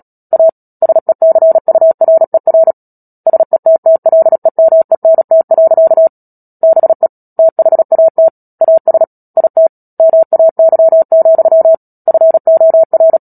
Morse code, heard as a keyed tone, about 37 wpm, 650 Hz. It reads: A SECURER SETTLEMENT. BE THAT AS IT MAY, FOR
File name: Book Two / 10-Chapter 10 - Epilogue - 36 WPM